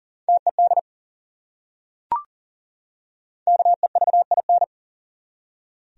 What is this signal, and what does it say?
Morse code, heard as a keyed tone, about 40 wpm, 700 Hz. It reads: TED KEVIN
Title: USA Common Names - 40wpm